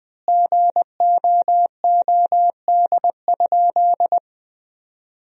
Morse code, heard as a keyed tone, about 20 wpm, 700 Hz. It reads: GOOD?
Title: 3x Word-Spacing - Sentences from Top 200 Words - 20wpm